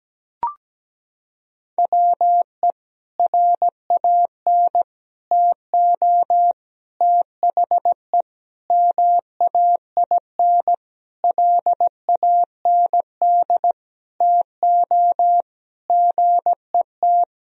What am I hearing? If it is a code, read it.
Morse code, heard as a keyed tone, about 17 wpm, 700 Hz. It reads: WE RAN TO THE MAINLAND TO GET
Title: Sentences from Top 1000 Words - Review Part 2 - 17wpm